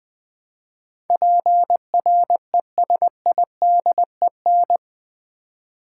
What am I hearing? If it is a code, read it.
Morse code, heard as a keyed tone, about 20 wpm, 700 Hz. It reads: PRESIDEN
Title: Build A Word - 4 Letters At A Time - 12-Letter Words - 20wpm